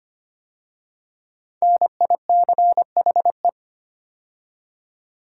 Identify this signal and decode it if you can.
Morse code, heard as a keyed tone, about 25 wpm, 700 Hz. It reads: NICHE
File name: 5 Letter Words - First 3 Letters Given - 25wpm